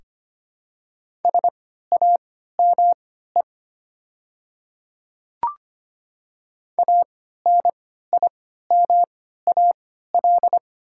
Morse code, heard as a keyed tone, about 25 wpm, spaced as for 14 wpm, 700 Hz. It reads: SAME ANIMAL
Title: 3x Character Spacing - Top 200 Words - 25wpm